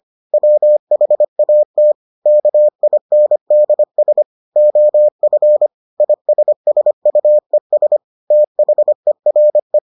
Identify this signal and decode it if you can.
Morse code, heard as a keyed tone, about 25 wpm, 600 Hz. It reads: WHAT KINDS OF ISSUES THERE